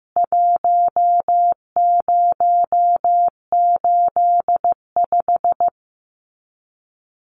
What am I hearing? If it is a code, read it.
Morse code, heard as a keyed tone, about 15 wpm, 700 Hz. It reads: 1085
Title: CWT Exchange 15wpm